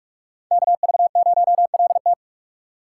Morse code, heard as a keyed tone, about 41 wpm, 700 Hz. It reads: KV0LT